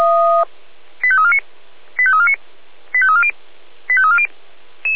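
Five-tone selective call (ZVEI), heard as quick runs of five tones, tones of about 70 ms. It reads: 85349 85349 85340 85340
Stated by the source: ZVEI five-tone selective call 85275 received on an SDR, logged by the Feuerwehr Orlamünde fire brigade.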